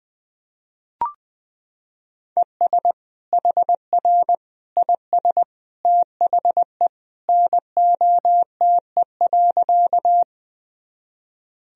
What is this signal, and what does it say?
Morse code, heard as a keyed tone, about 21 wpm, 700 Hz. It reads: ES HR IS THE NOTE.
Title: Sentences from Top 500 Words and Abbreviations 20wpm